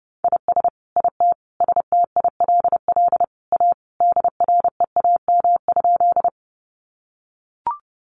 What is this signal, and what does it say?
Morse code, heard as a keyed tone, about 30 wpm, 700 Hz. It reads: IS IT STILL A DREAM?
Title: Sentences from Top 1500 Words 30wpm